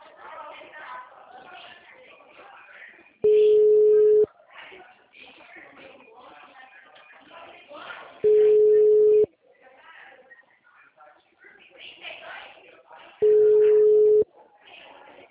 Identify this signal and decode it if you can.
Ringback tone (Europe, 425 Hz), with speech.
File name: out- -2009- - - .8489